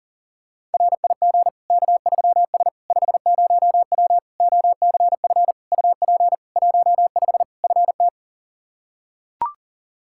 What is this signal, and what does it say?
Morse code, heard as a keyed tone, about 37 wpm, 700 Hz. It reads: RIG K3S 50W OCF UP 15 FT
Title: K1USN SST QSO Practice 40wpm